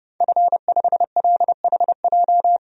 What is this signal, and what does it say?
Morse code, heard as a keyed tone, about 31 wpm, 700 Hz. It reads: F5LHJ
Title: DXCC-100 Call Signs - 5 Characters - Repeated 3x - 30wpm